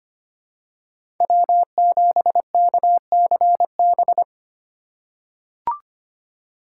Morse code, heard as a keyed tone, about 25 wpm, 700 Hz. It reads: W7KCB
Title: USA Call Signs - 3x Character Spacing - 25wpm